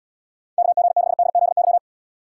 Morse code, heard as a keyed tone, about 78 wpm, 700 Hz. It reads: KG6DZY